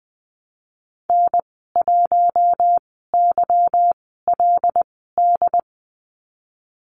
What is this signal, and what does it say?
Morse code, heard as a keyed tone, about 20 wpm, spaced as for 15 wpm, 700 Hz. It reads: N1YLD
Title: 200 Lesson 32g - Letter k - Call Signs 20wpm